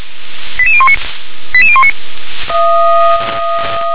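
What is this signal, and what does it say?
Five-tone selective call (ZVEI), heard as quick runs of five tones, tones of about 70 ms. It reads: 88019 88019
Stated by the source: ZVEI five-tone selective call received on SDR.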